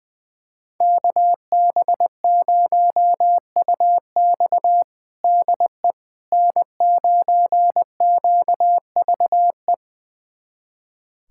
Morse code, heard as a keyed tone, about 21 wpm, 700 Hz. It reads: KB0UX DE N9QVE
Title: USA Callsign DE Callsign 20wpm